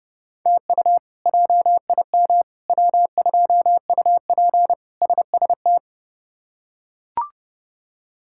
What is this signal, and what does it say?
Morse code, heard as a keyed tone, about 32 wpm, 700 Hz. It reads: TU JIM W2UP SST
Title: K1USN SST QSO Practice - 30wpm